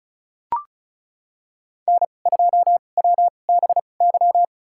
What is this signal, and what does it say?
Morse code, heard as a keyed tone, about 34 wpm, spaced as for 25 wpm, 700 Hz. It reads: N2WBY
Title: USA Call Signs - 5 Characters - 2x Character Spacing - 35wpm